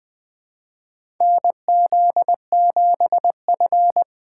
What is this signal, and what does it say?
Morse code, heard as a keyed tone, about 20 wpm, 700 Hz. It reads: N Z 7 F
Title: USA Call Signs - 4 Characters - 6x Character Spacing - 20wpm